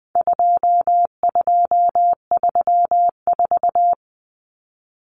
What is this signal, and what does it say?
Morse code, heard as a keyed tone, about 20 wpm, 700 Hz. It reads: 2234
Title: Common Cut Numbers - 4-Digit 20wpm